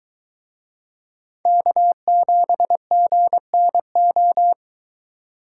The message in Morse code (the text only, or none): K7GNO